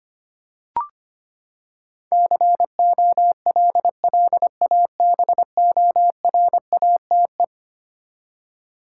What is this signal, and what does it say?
Morse code, heard as a keyed tone, about 25 wpm, 700 Hz. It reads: COLLABORATE
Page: morse-code-ninja-practice-25wpm